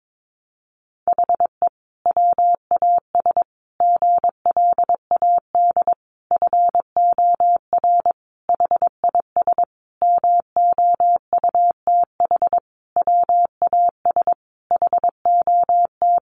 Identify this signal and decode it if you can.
Morse code, heard as a keyed tone, about 22 wpm, 700 Hz. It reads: HE WAS GLAD FOR HIS MOUTH WAS HOT
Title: Sentences from Top 1500 Words 22wpm